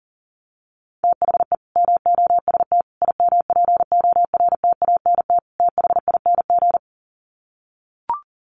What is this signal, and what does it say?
Morse code, heard as a keyed tone, about 40 wpm, 700 Hz. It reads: THE MOST IMPORTANT THING
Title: Sets of 4 Words 40wpm